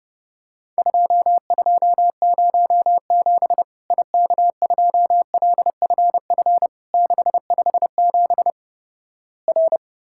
Morse code, heard as a keyed tone, about 31 wpm, 700 Hz. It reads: 2207 IK2LFF 657 R
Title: Worked All Europe DX Contest - QTC Exchange - 30wpm